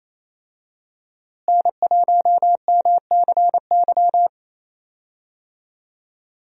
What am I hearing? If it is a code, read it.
Morse code, heard as a keyed tone, about 28 wpm, 700 Hz. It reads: N1MCY
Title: USA Call Signs - 5 Characters - Repeated 3x - 28wpm